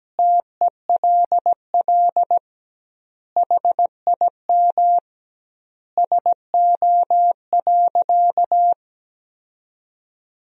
Morse code, heard as a keyed tone, about 17 wpm, 700 Hz. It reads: TELL HIM SO.
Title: 2x Word-Spacing - Sentences from Top 200 Words - 17wpm